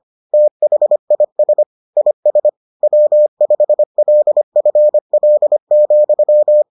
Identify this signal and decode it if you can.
Morse code, heard as a keyed tone, about 25 wpm, 600 Hz. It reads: THIS IS W5LFL,